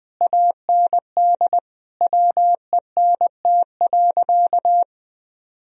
Morse code, heard as a keyed tone, about 20 wpm, 700 Hz. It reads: AND WENT.
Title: Sentences from Top 1000 Words - Review Part 2 - 20wpm